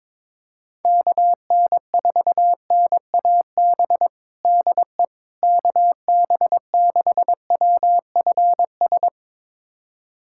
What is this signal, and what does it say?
Morse code, heard as a keyed tone, about 22 wpm, 700 Hz. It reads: KN4NAB DE KB6WFS
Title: USA Callsign DE Callsign - 6 Characters - 22wpm